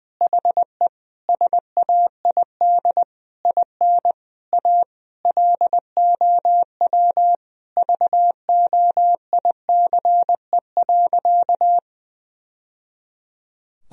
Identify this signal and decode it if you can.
Morse code, heard as a keyed tone, about 20 wpm, 700 Hz. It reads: HE SAID IN A LOW VOICE.
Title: Sentences from Top 1000 Words - Review Part 1 - 20wpm